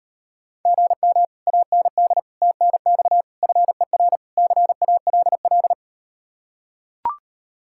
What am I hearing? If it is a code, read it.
Morse code, heard as a keyed tone, about 39 wpm, 700 Hz. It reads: GM AND TNX FER CALL